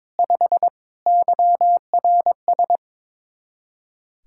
Morse code, heard as a keyed tone, about 22 wpm, 700 Hz. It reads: 5 YRS
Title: Practice QSO - Exchange 3 - More Info - 22wpm